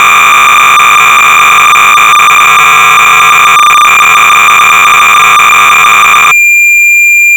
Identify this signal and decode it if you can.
Cassette data (Kansas City type).
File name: Lazer Fire-(Mark Yvanovich)(University Software Vol IV)-BASIC-Sorcerer-300baud